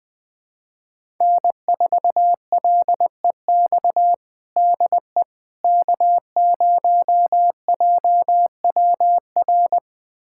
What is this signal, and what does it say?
Morse code, heard as a keyed tone, about 20 wpm, 700 Hz. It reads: N4LEX DE K0JWR
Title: USA Callsign DE Callsign - 5 Characters - 20wpm